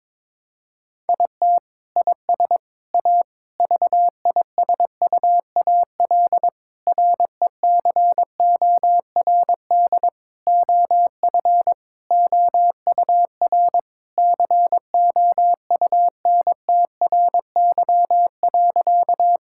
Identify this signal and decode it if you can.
Morse code, heard as a keyed tone, about 22 wpm, 700 Hz. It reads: IT IS A VISUAL RECORD OF OUR COUNTRY.
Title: Sentences from Top 1000 Words - Review Part 6 - 22wpm